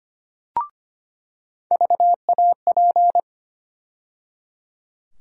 Morse code, heard as a keyed tone, about 25 wpm, 700 Hz. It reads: VAP
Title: USA Call Sign Postfixes - 3 Letters 25wpm